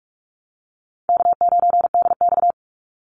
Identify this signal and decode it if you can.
Morse code, heard as a keyed tone, about 46 wpm, 700 Hz. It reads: K9DX
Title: Easy Call Signs - 45wpm